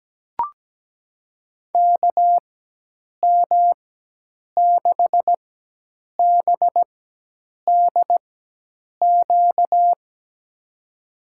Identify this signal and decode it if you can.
Morse code, heard as a keyed tone, about 17 wpm, spaced as for 8 wpm, 700 Hz. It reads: KM6BDQ
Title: USA Call Signs - 4x Character Spacing - 17wpm